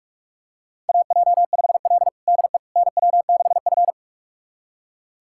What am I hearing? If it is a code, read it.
Morse code, heard as a keyed tone, about 42 wpm, 690 Hz. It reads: AJ5L DE NW6F
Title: USA Callsign DE Callsign - 4 Characters - 45wpm